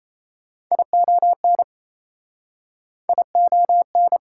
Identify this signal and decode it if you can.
Morse code, heard as a keyed tone, about 37 wpm, 690 Hz. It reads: ION ION